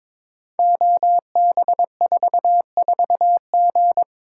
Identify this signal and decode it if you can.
Morse code, heard as a keyed tone, about 23 wpm, 700 Hz. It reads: OB44G
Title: Code Groups - 5 Letter:Numbers - 22wpm